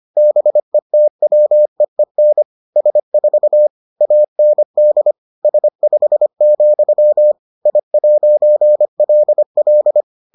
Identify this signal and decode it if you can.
Morse code, heard as a keyed tone, about 25 wpm, 600 Hz. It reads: BETWEEN S4 AND S5, I'LL